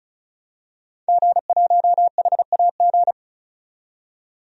Morse code, heard as a keyed tone, about 36 wpm, 700 Hz. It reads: G1HAG
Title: DXCC-100 Call Signs - 5 Characters - 35wpm